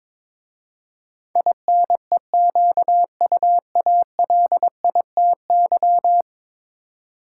Morse code, heard as a keyed tone, about 22 wpm, 700 Hz. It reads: INEQUALITY
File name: practice / 22wpm / 10 Letter Words - First 3 Letters Given - 22wpm